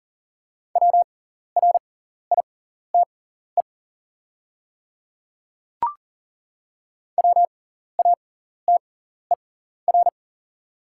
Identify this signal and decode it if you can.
Morse code, heard as a keyed tone, about 40 wpm, spaced as for 14 wpm, 690 Hz. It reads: WRITE WATER
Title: 6x Character Spacing - Top 100 Words - 40wpm